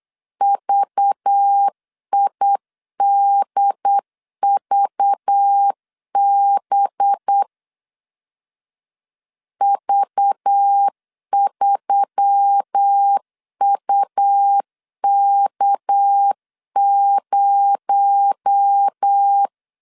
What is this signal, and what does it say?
Morse code, heard as a keyed tone, about 8 wpm, 800 Hz. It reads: VIDVB V3UK0
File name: CW / DET / 1 6 1500C S B 8wpm 5142